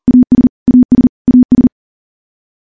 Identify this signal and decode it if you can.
Morse code, heard as a keyed tone, about 40 wpm, 250 Hz. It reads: AS AS AS
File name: morse / morse-training-40WPM